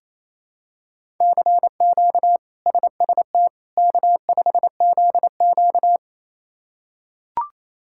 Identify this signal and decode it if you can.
Morse code, heard as a keyed tone, about 28 wpm, 700 Hz. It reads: CQ SST K5ZQ
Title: K1USN SST QSO Practice - 28wpm